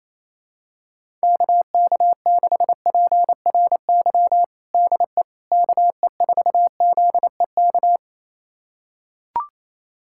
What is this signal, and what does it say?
Morse code, heard as a keyed tone, about 27 wpm, 700 Hz. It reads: KK6PRY DE KE4ZEK
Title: USA Callsign DE Callsign - 6 Characters - 28wpm